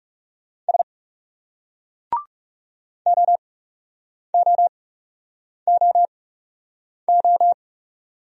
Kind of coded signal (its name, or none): Morse code